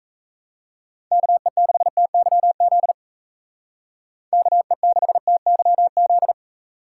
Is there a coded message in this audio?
Yes: Morse code from 1.1 to 6.3 s, heard as a keyed tone, about 36 wpm, 700 Hz. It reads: KE6TYZ KE6TYZ